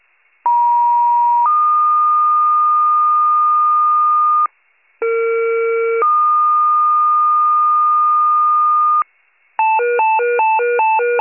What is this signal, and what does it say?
Two-tone page, heard as a one-second tone, then a three-second tone, about 950 then 1220 Hz, about 460 then 1220 Hz. It is heard from 0.5 to 9.0 s.